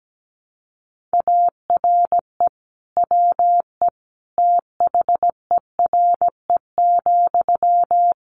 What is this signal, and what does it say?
Morse code, heard as a keyed tone, about 17 wpm, 700 Hz. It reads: ARE WE THERE,
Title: Sentences from Top 1300 Words 17wpm